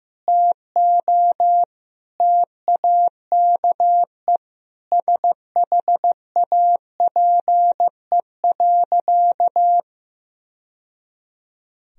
Morse code, heard as a keyed tone, about 15 wpm, 700 Hz. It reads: TO TAKE SHAPE.
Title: Sentences from Top 1000 Words - Review Part 3 - 15wpm